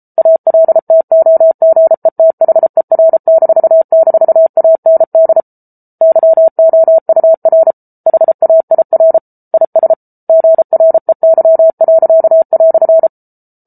Morse code, heard as a keyed tone, about 34 wpm, 650 Hz. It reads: ALTOGETHER--AND YOUR HAIR IS GREY."